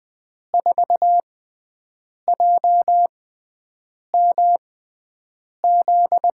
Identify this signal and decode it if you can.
Morse code, heard as a keyed tone, about 21 wpm, spaced as for 7 wpm, 700 Hz. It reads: 4JMZ
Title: USA Call Signs - 6 Characters - 6x Character Spacing - 20wpm